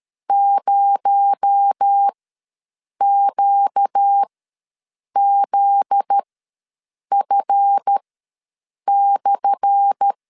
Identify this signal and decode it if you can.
Morse code, heard as a keyed tone, about 13 wpm, spaced as for 7 wpm, 800 Hz. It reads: 0QZF/
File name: LEZIONI / 10 SIGNS / 10A SLASH ws 12wpm 3443